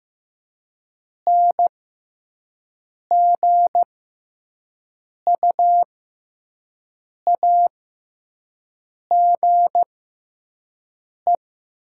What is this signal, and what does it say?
Morse code, heard as a keyed tone, about 15 wpm, spaced as for 5 wpm, 690 Hz. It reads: NGUAGE
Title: 6x Character Spacing - Top 500 Words - 15wpm